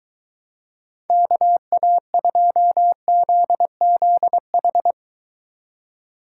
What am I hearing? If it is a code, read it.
Morse code, heard as a keyed tone, about 23 wpm, 700 Hz. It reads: KA2ZZH